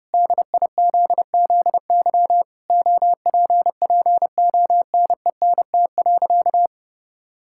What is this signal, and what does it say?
Morse code, heard as a keyed tone, about 30 wpm, 700 Hz. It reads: DIZZY OPPONENT.
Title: Pangram Challenge v2 - 30wpm